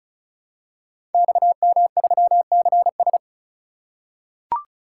Morse code, heard as a keyed tone, about 36 wpm, 700 Hz. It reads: XM3CS
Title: Code Groups - 5 Letter:Numbers - 35wpm